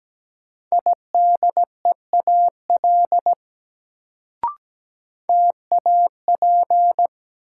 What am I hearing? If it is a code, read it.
Morse code, heard as a keyed tone, about 17 wpm, 700 Hz. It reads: IDEAL TAP
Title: Sentences from Top 1400 Words - New Word Drill - 17wpm